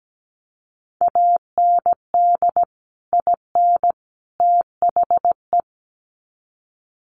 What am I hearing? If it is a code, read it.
Morse code, heard as a keyed tone, about 17 wpm, 700 Hz. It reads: AND IN THE
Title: Sets of 3 Words 17wpm